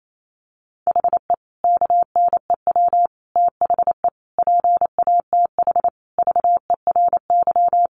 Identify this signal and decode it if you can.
Morse code, heard as a keyed tone, about 28 wpm, 700 Hz. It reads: HE KNEW THE PATH VERY
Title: Sentences from Top 1400 Words 28wpm